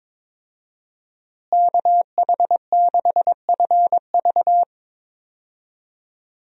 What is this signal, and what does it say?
Morse code, heard as a keyed tone, about 22 wpm, 700 Hz. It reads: KH6FV
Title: USA Call Signs - 5 Characters - Repeated 2x - 22wpm